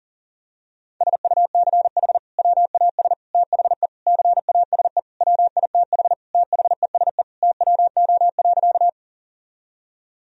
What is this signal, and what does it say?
Morse code, heard as a keyed tone, about 37 wpm, 700 Hz. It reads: SUCH WAS THE CASE WITH THESE TWO.
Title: Sentences from Top 1000 Words - Review Part 1 - 40wpm